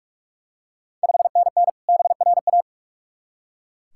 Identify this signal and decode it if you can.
Morse code, heard as a keyed tone, about 44 wpm, 690 Hz. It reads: 5NN BRA
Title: Ontario QSO Party Exchange 45wpm